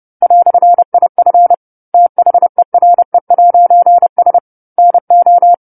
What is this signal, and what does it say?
Morse code, heard as a keyed tone, about 30 wpm, 700 Hz. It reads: "IF THERE'S NO
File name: 10-Chapter-10 30 WPM